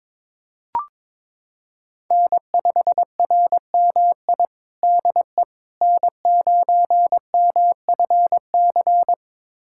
Morse code, heard as a keyed tone, about 22 wpm, 700 Hz. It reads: N5RMI DE N9MFC
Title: USA Callsign DE Callsign - 5 Characters - 22wpm